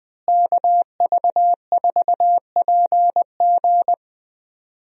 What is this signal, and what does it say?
Morse code, heard as a keyed tone, about 21 wpm, 700 Hz. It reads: KV4PG
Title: USA Call Signs - 5 Characters - Repeated 3x - 20wpm